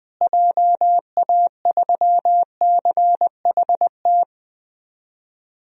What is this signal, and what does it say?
Morse code, heard as a keyed tone, about 20 wpm, 700 Hz. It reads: JA3CHT
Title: DXCC-100 Call Signs - 6 Characters - Repeated 3x - 20wpm